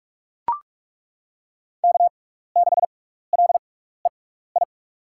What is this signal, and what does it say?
Morse code, heard as a keyed tone, about 42 wpm, spaced as for 15 wpm, 700 Hz. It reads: K6LEI
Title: USA Call Signs - 5 Characters - 6x Character Spacing - 45wpm